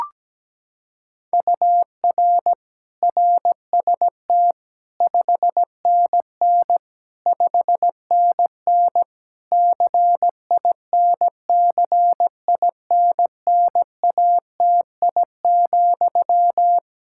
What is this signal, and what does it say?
Morse code, heard as a keyed tone, about 17 wpm, 700 Hz. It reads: UR RST 5NN 5NN CINCINNATI,
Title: Practice QSO - Exchange 1 - Beginning - 17wpm